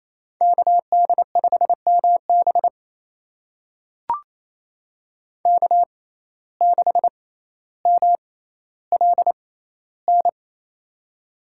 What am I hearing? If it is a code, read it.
Morse code, heard as a keyed tone, about 28 wpm, spaced as for 14 wpm, 700 Hz. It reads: KD5MB K6MLN
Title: USA Call Signs - 5 Characters - 6x Character Spacing - 28wpm